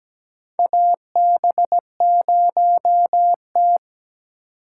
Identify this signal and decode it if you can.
Morse code, heard as a keyed tone, about 17 wpm, 700 Hz. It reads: AB0T
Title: USA Call Signs - 4 Characters - 3x Character Spacing - 17wpm